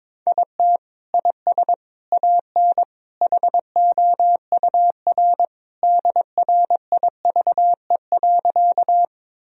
Morse code, heard as a keyed tone, about 22 wpm, 700 Hz. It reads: IT IS AN HOUR DRIVE.
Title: Sentences from Top 1000 Words - Review Part 2 - 22wpm